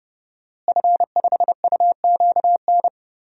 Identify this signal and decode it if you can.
Morse code, heard as a keyed tone, about 30 wpm, 700 Hz. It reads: F5UQN
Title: DXCC-100 Call Signs - 5 Characters - Repeated 3x - 30wpm